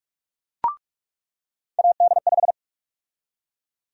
Morse code, heard as a keyed tone, about 47 wpm, 700 Hz. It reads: AD5
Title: USA Call Sign Prefixes - 45wpm